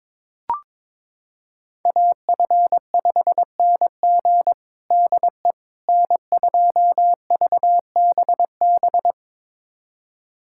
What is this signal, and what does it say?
Morse code, heard as a keyed tone, about 22 wpm, 700 Hz. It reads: AF5NG DE N2VBB
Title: USA Callsign DE Callsign - 5 Characters - 22wpm